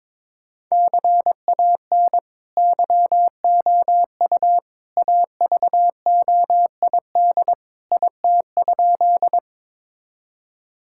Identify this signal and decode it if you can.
Morse code, heard as a keyed tone, about 22 wpm, 700 Hz. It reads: CAN YOU AVOID IT?
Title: Sentences from Top 1000 Words - Review Part 6 - 22wpm